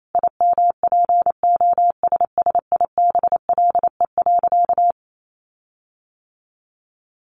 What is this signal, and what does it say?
Morse code, heard as a keyed tone, about 28 wpm, 700 Hz. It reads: IMPOSSIBLE.
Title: Sentences from Top 1300 Words 28wpm